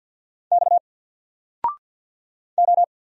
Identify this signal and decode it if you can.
Morse code, heard as a keyed tone, about 50 wpm, 700 Hz. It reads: X O
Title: Single Letter-Number 50wpm